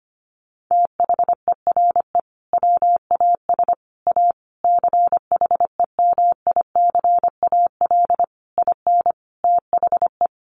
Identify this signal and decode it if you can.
Morse code, heard as a keyed tone, about 25 wpm, 700 Hz. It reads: THERE WAS A CHEMICAL IN THE
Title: Sentences from Top 1200 Words 25wpm